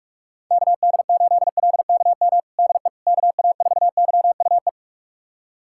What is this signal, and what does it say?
Morse code, heard as a keyed tone, about 46 wpm, 700 Hz. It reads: KD8LKM DE KA4YUE
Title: USA Callsign DE Callsign - 6 Characters - 45wpm